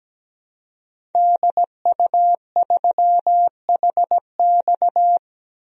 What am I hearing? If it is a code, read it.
Morse code, heard as a keyed tone, about 17 wpm, 700 Hz. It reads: DU3HX
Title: DXCC-100 Call Signs - 5 Characters - Repeated 2x - 17wpm